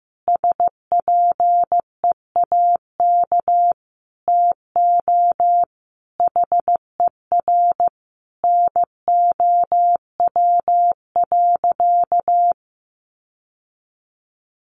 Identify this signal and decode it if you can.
Morse code, heard as a keyed tone, about 15 wpm, 700 Hz. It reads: SPEAK TO HER NOW.
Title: Sentences from Top 1400 Words 15wpm